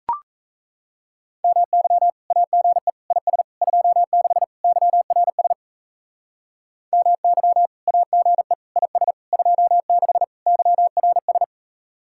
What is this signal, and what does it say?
Morse code, heard as a keyed tone, about 34 wpm, 700 Hz. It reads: MY AGE IS 26 YRS MY AGE IS 26 YRS
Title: Speed-Racing Common QSO Phrases - 28wpm